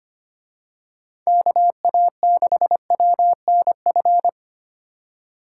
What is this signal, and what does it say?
Morse code, heard as a keyed tone, about 29 wpm, 700 Hz. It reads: KA6WNF
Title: Speed Racing - USA 2x3 Call Signs 25wpm